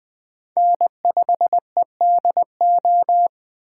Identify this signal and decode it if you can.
Morse code, heard as a keyed tone, about 23 wpm, 700 Hz. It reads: N5EDO
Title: Speed Racing - USA Call Signs 20wpm